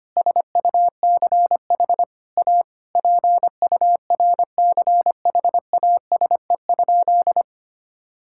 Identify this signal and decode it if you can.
Morse code, heard as a keyed tone, about 25 wpm, 700 Hz. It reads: SUCH A PURCHASE?
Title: Sentences from Top 1000 Words - Review Part 6 - 25wpm